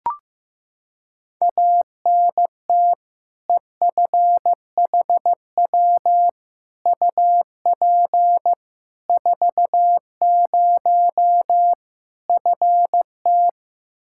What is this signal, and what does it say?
Morse code, heard as a keyed tone, about 16 wpm, 700 Hz. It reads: ANT EFHW UP 40 FT
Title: Practice QSO - Exchange 2 - Info - 15wpm